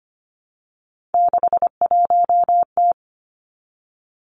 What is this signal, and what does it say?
Morse code, heard as a keyed tone, about 25 wpm, 700 Hz. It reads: 61T
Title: Three Challenging Letter-Number Combinations 25wpm